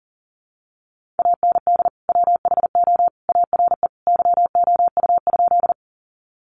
Morse code, heard as a keyed tone, about 40 wpm, 700 Hz. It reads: AND WHO ARE YOU?
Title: Sentences from Top 100 Words 40wpm